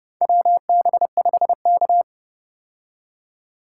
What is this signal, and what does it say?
Morse code, heard as a keyed tone, about 29 wpm, 700 Hz. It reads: WB5K
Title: K1USN SST QSO Practice - 30wpm